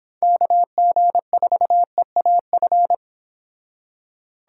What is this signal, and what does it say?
Morse code, heard as a keyed tone, about 28 wpm, 700 Hz. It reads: KG4EAF